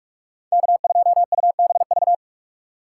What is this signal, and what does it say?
Morse code, heard as a keyed tone, about 46 wpm, 700 Hz. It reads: K2UBV
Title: USA Call Signs - 5 Characters - 2x Character Spacing - 45wpm